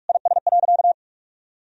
Morse code, heard as a keyed tone, about 47 wpm, 700 Hz. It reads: IS.
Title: Sentences from Top 1000 Words - Review Part 4 - 45wpm